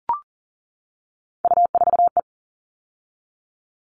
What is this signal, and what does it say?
Morse code, heard as a keyed tone, about 40 wpm, 700 Hz. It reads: U4E
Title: All Cut Numbers - 3-Digit 40wpm